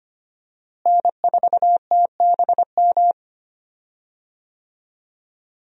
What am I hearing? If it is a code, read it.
Morse code, heard as a keyed tone, about 25 wpm, 700 Hz. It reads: N4TBM
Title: USA Call Signs - 5 Characters - Repeated 3x - 25wpm